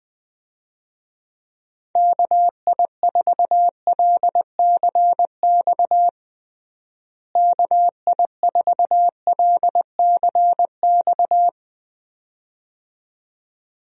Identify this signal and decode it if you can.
Morse code, heard as a keyed tone, about 20 wpm, 700 Hz. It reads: KI4LCX KI4LCX
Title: USA Call Signs - 6 Characters - Repeated 2x - 20wpm